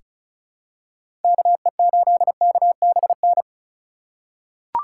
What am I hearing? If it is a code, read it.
Morse code, heard as a keyed tone, about 36 wpm, 700 Hz. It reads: KE8KBN
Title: USA Call Signs - 6 Characters - 3x Character Spacing - 35wpm